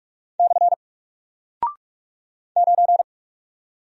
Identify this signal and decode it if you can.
Morse code, heard as a keyed tone, about 43 wpm, 700 Hz. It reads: / 9
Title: Single Character (Letter:Number:Punctuation:Prosign) Alt 45wpm